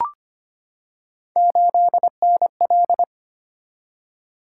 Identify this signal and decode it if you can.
Morse code, heard as a keyed tone, about 25 wpm, 700 Hz. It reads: 8NL
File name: Three Letter-Number Combinations - 25wpm